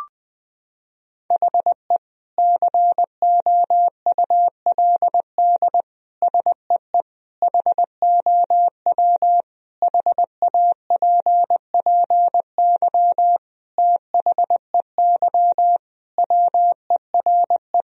Morse code, heard as a keyed tone, about 20 wpm, 700 Hz. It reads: HE COULD SEE HOW HAPPY THEY WERE
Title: Sentences from Top 1000 Words - Review Part 3 - 20wpm